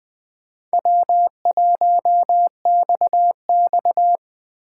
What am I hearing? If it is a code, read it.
Morse code, heard as a keyed tone, about 24 wpm, 700 Hz. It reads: W1XX
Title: Speed Racing - USA Call Signs 20wpm